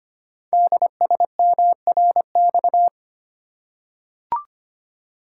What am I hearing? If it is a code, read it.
Morse code, heard as a keyed tone, about 25 wpm, 700 Hz. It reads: DSMRX
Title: Code Groups - 5 Letters - 25wpm